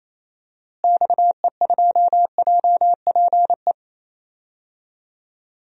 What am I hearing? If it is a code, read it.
Morse code, heard as a keyed tone, about 28 wpm, 700 Hz. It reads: XE2JPE
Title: DXCC-100 Call Signs - 6 Characters - Repeated 2x - 28wpm